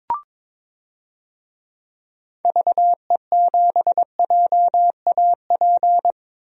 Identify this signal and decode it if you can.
Morse code, heard as a keyed tone, about 21 wpm, 700 Hz. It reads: VE7JAP
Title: DXCC-100 Call Signs - 6 Characters - 22wpm